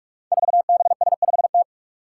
Morse code, heard as a keyed tone, about 44 wpm, 700 Hz. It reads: 4BS5T
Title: Five Challenging Letter-Number Combinations 45wpm